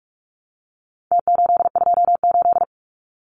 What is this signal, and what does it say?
Morse code, heard as a keyed tone, about 46 wpm, 700 Hz. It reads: T828
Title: Numbers - Contest Style - 4-Digit 45wpm